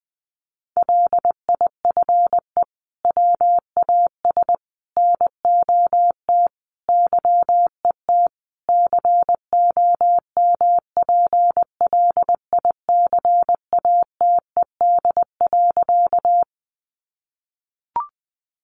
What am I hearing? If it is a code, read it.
Morse code, heard as a keyed tone, about 20 wpm, 700 Hz. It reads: LIFE WAS NOT YET COMPLICATED.
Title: Sentences from Top 1200 Words 20wpm